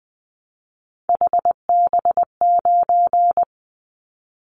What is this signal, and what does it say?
Morse code, heard as a keyed tone, about 20 wpm, 700 Hz. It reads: HB9
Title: Callsign Prefixes for Top 25 DXCC - 20wpm